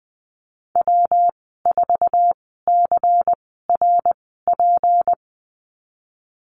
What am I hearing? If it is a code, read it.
Morse code, heard as a keyed tone, about 20 wpm, spaced as for 15 wpm, 700 Hz. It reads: W4CRP
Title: 154 Lesson 25g - Letter p - Call Signs 20wpm